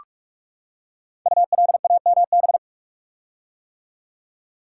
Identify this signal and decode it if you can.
Morse code, heard as a keyed tone, about 44 wpm, 700 Hz. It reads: ULAMB
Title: Code Groups - 5 Letters - 45wpm